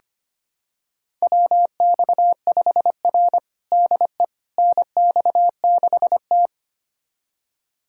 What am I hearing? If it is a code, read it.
Morse code, heard as a keyed tone, about 25 wpm, 700 Hz. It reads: WX5R DE NX6T
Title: USA Callsign DE Callsign - 4 Characters - 25wpm